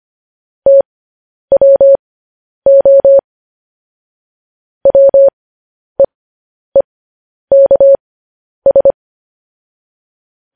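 Morse code, heard as a keyed tone, about 25 wpm, spaced as for 10 wpm, 550 Hz. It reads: TWO WEEKS
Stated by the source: Morse code practice of news headlines at 10 wpm.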